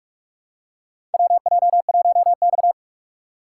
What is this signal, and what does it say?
Morse code, heard as a keyed tone, about 44 wpm, 700 Hz. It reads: WJ1X
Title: USA Call Signs - 4 Characters - 6x Character Spacing - 45wpm